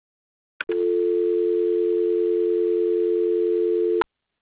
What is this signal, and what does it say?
Dial tone (North America), 350 + 440 Hz.